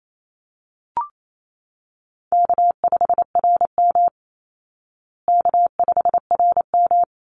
Morse code, heard as a keyed tone, about 28 wpm, 700 Hz. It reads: K5RM K5RM
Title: Easy Call Signs Repeated 2x - 28wpm